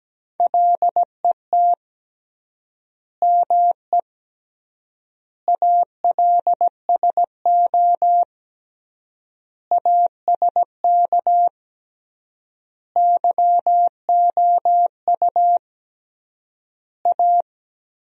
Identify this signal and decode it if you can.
Morse code, heard as a keyed tone, about 17 wpm, 700 Hz. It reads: LET ME ALSO ASK YOU A
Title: 3x Word-Spacing - Sentences from Top 400 Words - 17wpm